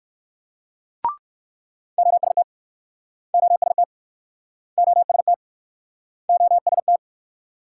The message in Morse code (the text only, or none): OST OST OST OST